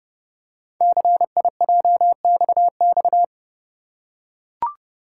Morse code, heard as a keyed tone, about 28 wpm, 700 Hz. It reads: CIJXX
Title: Code Groups - 5 Letters - 30wpm